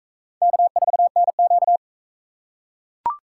Morse code, heard as a keyed tone, about 40 wpm, 700 Hz. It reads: K4NQ